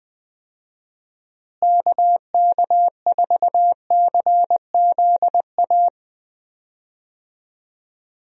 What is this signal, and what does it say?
Morse code, heard as a keyed tone, about 20 wpm, 700 Hz. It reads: KK4CZA
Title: USA Call Signs - 6 Characters - Repeated 3x - 20wpm